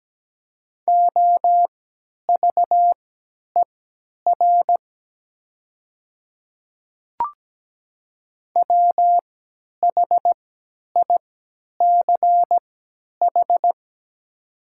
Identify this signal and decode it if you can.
Morse code, heard as a keyed tone, about 17 wpm, spaced as for 10 wpm, 700 Hz. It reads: OVER WHICH
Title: 3x Character Spacing - Top 100 Words - 17wpm